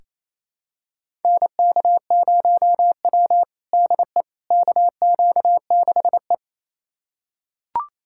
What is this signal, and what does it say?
Morse code, heard as a keyed tone, about 28 wpm, 700 Hz. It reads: NK0W DE KQ6E
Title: USA Callsign DE Callsign - 4 Characters - 28wpm